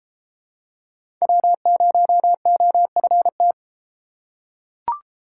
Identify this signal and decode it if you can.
Morse code, heard as a keyed tone, about 32 wpm, 700 Hz. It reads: W0OFT